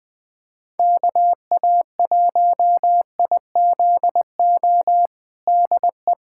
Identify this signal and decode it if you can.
Morse code, heard as a keyed tone, about 20 wpm, 700 Hz. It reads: KA1IZO DE
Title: USA Callsign DE Callsign - 6 Characters - 20wpm